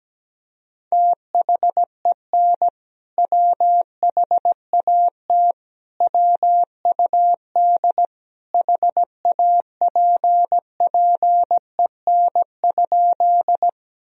Morse code, heard as a keyed tone, about 17 wpm, 700 Hz. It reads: THEN WHAT WUD HAPPEN?
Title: Sentences from Top 400 Words and Abbreviations 17wpm